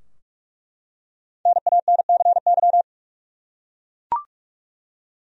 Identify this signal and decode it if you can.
Morse code, heard as a keyed tone, about 46 wpm, 700 Hz. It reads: NANCY
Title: USA Common Names - 45wpm